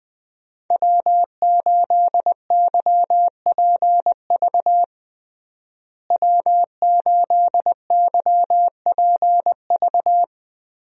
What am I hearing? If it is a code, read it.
Morse code, heard as a keyed tone, about 20 wpm, 700 Hz. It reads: W8YPV W8YPV
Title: USA Call Signs - 5 Characters - Repeated 2x - 20wpm